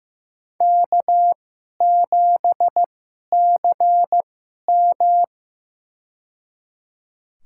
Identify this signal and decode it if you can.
Morse code, heard as a keyed tone, about 15 wpm, spaced as for 11 wpm, 700 Hz. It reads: K7CM
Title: USA Call Signs - 4 Characters - 2x Character Spacing - 15wpm